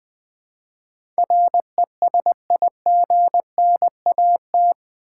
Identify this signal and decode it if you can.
Morse code, heard as a keyed tone, about 20 wpm, 700 Hz. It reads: RESIGNAT
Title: Build A Word - 4 Letters At A Time - 11-Letter Words - 20wpm